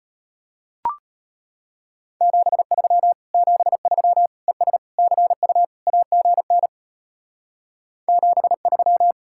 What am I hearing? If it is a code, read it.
Morse code, heard as a keyed tone, about 30 wpm, 700 Hz. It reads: 73 73 ES CU AGN 73
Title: Speed-Racing Common QSO Phrases Encore - 25wpm